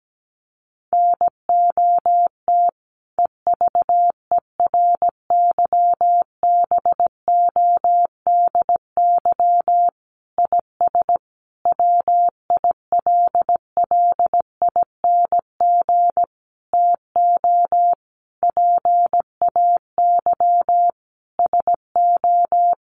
Morse code, heard as a keyed tone, about 17 wpm, 700 Hz. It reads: NOT EVERYBODY IS WILLING TO PAY SO
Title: Sentences from Top 1400 Words 17wpm